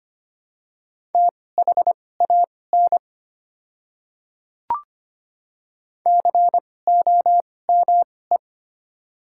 Morse code, heard as a keyed tone, about 25 wpm, spaced as for 18 wpm, 700 Hz. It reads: THAN COME
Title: 2x Character Spacing - Top 100 Words - 25wpm